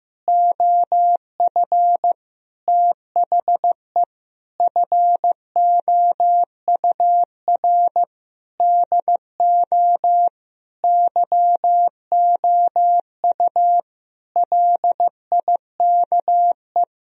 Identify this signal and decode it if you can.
Morse code, heard as a keyed tone, about 15 wpm, 700 Hz. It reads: OF THE FOUR DO YOU LIKE
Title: Sentences from Top 1000 Words - Review Part 1 - 15wpm